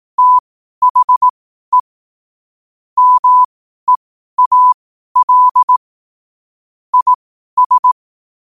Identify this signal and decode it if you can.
Morse code, heard as a keyed tone, about 18 wpm, spaced as for 12 wpm, 1000 Hz. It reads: THE MEAL IS
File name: qotdfiles / 12wpm